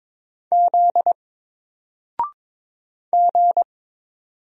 Morse code, heard as a keyed tone, about 22 wpm, 700 Hz. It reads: Z G
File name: Single-Letter-Number-22wpm